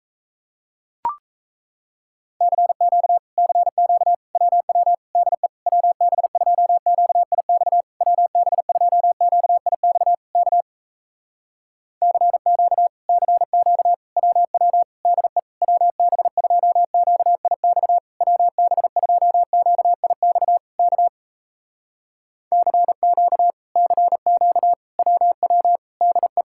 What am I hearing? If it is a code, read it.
Morse code, heard as a keyed tone, about 38 wpm, 700 Hz. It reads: CQ CQ WW DE WB2QIX WB2QIX K CQ CQ WW DE WB2QIX WB2QIX K CQ CQ WW DE